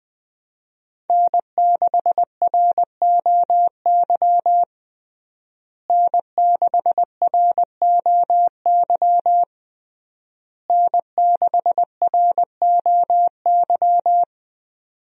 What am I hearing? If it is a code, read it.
Morse code, heard as a keyed tone, about 20 wpm, 700 Hz. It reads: N6ROY N6ROY N6ROY
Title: USA Call Signs - 5 Characters - Repeated 3x - 20wpm